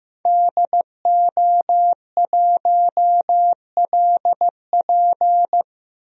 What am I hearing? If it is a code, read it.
Morse code, heard as a keyed tone, about 15 wpm, 700 Hz. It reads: DO1LP
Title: DXCC-100 Call Signs - 5 Characters - Repeated 2x - 15wpm